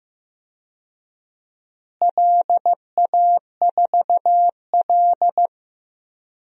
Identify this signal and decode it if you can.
Morse code, heard as a keyed tone, about 15 wpm, 700 Hz. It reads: LA4L
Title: DXCC-100 Call Signs - 4 Characters - Repeated 3x - 15wpm